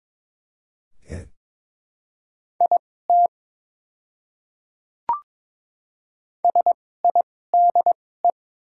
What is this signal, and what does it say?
Morse code, heard as a keyed tone, about 21 wpm, spaced as for 16 wpm, 700 Hz. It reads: IT SIDE
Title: 2x Character Spacing - Top 100 Words - 22wpm